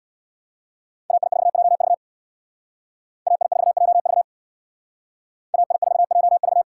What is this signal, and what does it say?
Morse code, heard as a keyed tone, about 63 wpm, 700 Hz. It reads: AI4JV AI4JV AI4JV